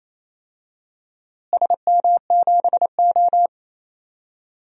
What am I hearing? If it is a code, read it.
Morse code, heard as a keyed tone, about 28 wpm, 700 Hz. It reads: SM7O
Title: DXCC-100 Call Signs - 4 Characters - Repeated 3x - 28wpm